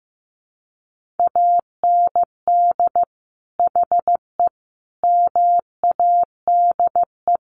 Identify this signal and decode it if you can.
Morse code, heard as a keyed tone, about 15 wpm, 700 Hz. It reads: AND HE MADE
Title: Sentences from Top 1500 Words 15wpm